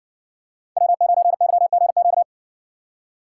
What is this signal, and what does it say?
Morse code, heard as a keyed tone, about 60 wpm, 700 Hz. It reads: W9YGQ